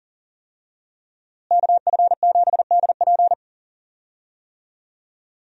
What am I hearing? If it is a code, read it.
Morse code, heard as a keyed tone, about 40 wpm, 700 Hz. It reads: KF7DP
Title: USA Call Signs - 5 Characters - Repeated 2x - 40wpm